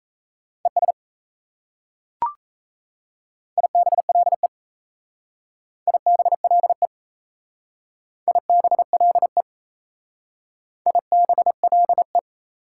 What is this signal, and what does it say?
Morse code, heard as a keyed tone, about 35 wpm, 690 Hz. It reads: ES IBLE IBLE IBLE IBLE